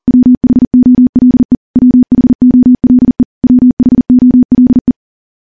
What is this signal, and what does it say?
Morse code, heard as a keyed tone, about 40 wpm, 250 Hz. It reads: WHOLE WHOLE WHOLE
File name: morse / morse-training-40WPM